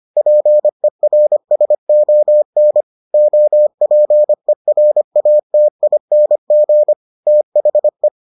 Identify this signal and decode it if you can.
Morse code, heard as a keyed tone, about 25 wpm, 600 Hz. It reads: PERSON OPERATING THE